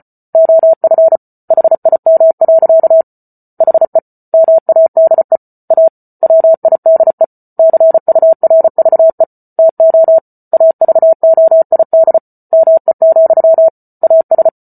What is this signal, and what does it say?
Morse code, heard as a keyed tone, about 34 wpm, 650 Hz. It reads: OF HIM. HE MADE A WIDE CURVE TO AVOID ME, AS